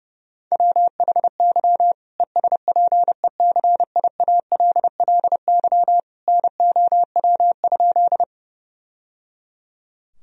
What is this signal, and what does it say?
Morse code, heard as a keyed tone, about 32 wpm, 700 Hz. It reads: WHY ESPECIALLY NOW?
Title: Sentences from Top 1000 Words - Review Part 4 - 30wpm